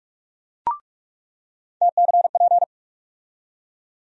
Morse code, heard as a keyed tone, about 46 wpm, 700 Hz. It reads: TCP
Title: USA Call Sign Postfixes - 3 Letters 45wpm